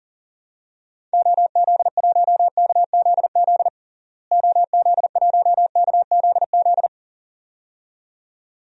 Morse code, heard as a keyed tone, about 40 wpm, 700 Hz. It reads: OZ1KZZ OZ1KZZ
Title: DXCC-100 Call Signs - 6 Characters - Repeated 2x - 40wpm